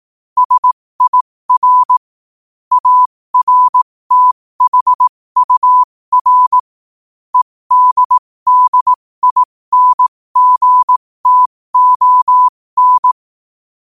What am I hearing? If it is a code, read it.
Morse code, heard as a keyed tone, about 18 wpm, spaced as for 15 wpm, 1000 Hz. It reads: SIR ARTHUR EDDINGTON
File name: qotdfiles / 15wpm